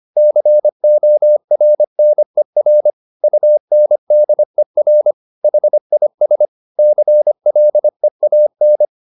Morse code, heard as a keyed tone, about 25 wpm, 600 Hz. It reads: CORNER UNDER HIS CLEAN
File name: the old man and the sea part 1 25wpm